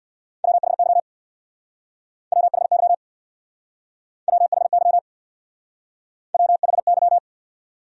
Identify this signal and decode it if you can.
Morse code, heard as a keyed tone, about 65 wpm, 700 Hz. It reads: WHY WHY WHY WHY